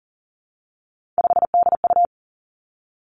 Morse code, heard as a keyed tone, about 38 wpm, 700 Hz. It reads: 5DU
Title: Three Challenging Letter-Number Combinations 40wpm